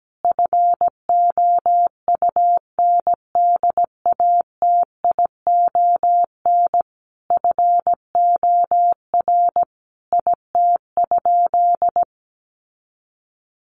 Morse code, heard as a keyed tone, about 17 wpm, 700 Hz. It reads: FOUNDATION FOR IT?
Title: Sentences from Top 1200 Words 17wpm